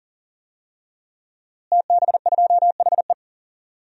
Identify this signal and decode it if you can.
Morse code, heard as a keyed tone, about 40 wpm, 700 Hz. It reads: TB2HE
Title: DXCC-100 Call Signs - 5 Characters - Repeated 3x - 40wpm